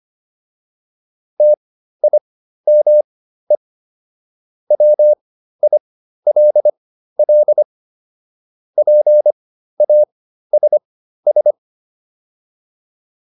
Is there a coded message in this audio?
Yes: Morse code from 1.4 to 11.5 s, heard as a keyed tone, about 25 wpm, spaced as for 13 wpm, 600 Hz. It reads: TIME WILL PASS